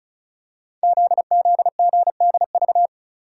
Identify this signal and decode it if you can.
Morse code, heard as a keyed tone, about 36 wpm, 700 Hz. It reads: ZZGDV
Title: Code Groups - 5 Letters - 35wpm